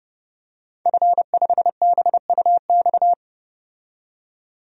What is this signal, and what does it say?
Morse code, heard as a keyed tone, about 29 wpm, 700 Hz. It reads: F5BUX
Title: DXCC-100 Call Signs - 5 Characters - 30wpm